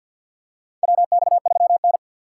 Morse code, heard as a keyed tone, about 51 wpm, 700 Hz. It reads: WX3N
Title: USA Call Signs - 4x Character Spacing - 50wpm